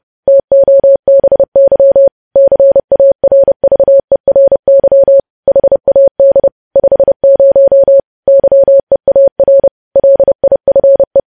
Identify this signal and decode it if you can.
Morse code, heard as a keyed tone, about 30 wpm, 550 Hz. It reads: TOBY CARVERY HAD 50 YEAR LIFE
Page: news headlines in morse code at 30 wpm